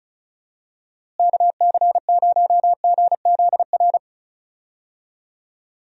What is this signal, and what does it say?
Morse code, heard as a keyed tone, about 36 wpm, 700 Hz. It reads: KC0GZR
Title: USA Call Signs - 6 Characters - Repeated 2x - 35wpm